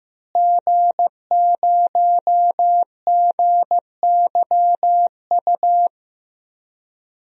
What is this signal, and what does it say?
Morse code, heard as a keyed tone, about 15 wpm, 700 Hz. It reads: G0GYU
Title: DXCC-100 Call Signs - 5 Characters - Repeated 3x - 15wpm